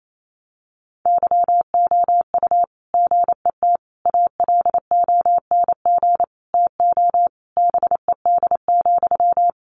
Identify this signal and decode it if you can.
Morse code, heard as a keyed tone, about 28 wpm, 700 Hz. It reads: YOU GET ALONG TO BED,
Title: Sentences from Top 700 Words 28wpm